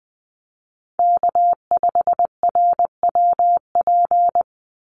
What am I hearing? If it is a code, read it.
Morse code, heard as a keyed tone, about 20 wpm, 700 Hz. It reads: K5RWP
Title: NA SOTA Chaser Callsigns 20wpm